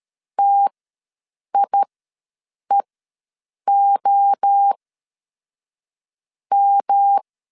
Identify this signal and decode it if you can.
Morse code, heard as a keyed tone, about 13 wpm, spaced as for 6 wpm, 800 Hz. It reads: TIEOM
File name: 7 GF9Z ws 12wpm